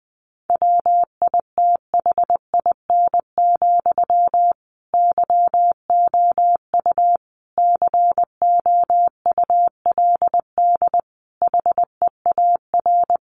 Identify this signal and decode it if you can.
Morse code, heard as a keyed tone, about 20 wpm, 700 Hz. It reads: WITHIN, YOU COULD HEAR
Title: A Sound Of Thunder 20wpm